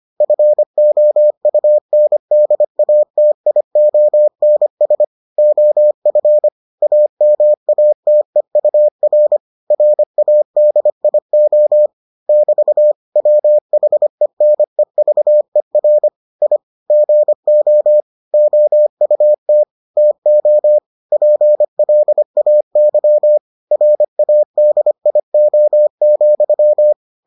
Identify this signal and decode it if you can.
Morse code, heard as a keyed tone, about 25 wpm, 600 Hz. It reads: FOUNDATIONS OF AMATEUR RADIO <BT> WHENEVER I GO OUT TO PLAY RADIO,